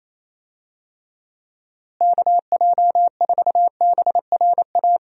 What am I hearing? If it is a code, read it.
Morse code, heard as a keyed tone, about 27 wpm, 700 Hz. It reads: KJ4BRA
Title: USA Call Signs - 6 Characters - Repeated 3x - 28wpm